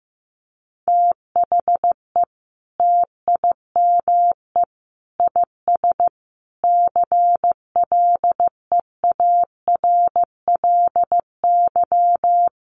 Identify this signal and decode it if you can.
Morse code, heard as a keyed tone, about 15 wpm, 700 Hz. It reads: THE TIME IS CLEARLY
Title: Sentences from Top 1300 Words 15wpm